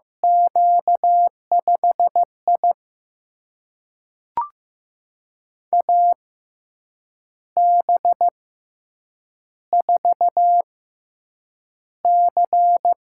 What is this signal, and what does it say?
Morse code, heard as a keyed tone, about 15 wpm, spaced as for 7 wpm, 700 Hz. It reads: Q5I AB4C
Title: USA Call Signs - 4 Characters - 6x Character Spacing - 15wpm